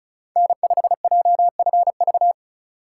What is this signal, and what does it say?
Morse code, heard as a keyed tone, about 36 wpm, 700 Hz. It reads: N5JFV
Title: USA Call Signs - 5 Characters - 4x Character Spacing - 35wpm